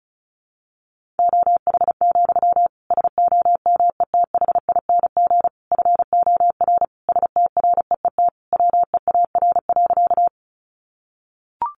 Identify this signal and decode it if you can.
Morse code, heard as a keyed tone, about 34 wpm, 700 Hz. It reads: OH, SOMETHING FOR STREET WEAR.
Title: Sentences from Top 1500 Words 35wpm